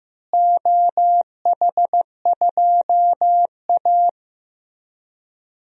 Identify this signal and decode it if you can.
Morse code, heard as a keyed tone, about 15 wpm, 700 Hz. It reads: OH2A
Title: DXCC-100 Call Signs - 4 Characters - Repeated 3x - 15wpm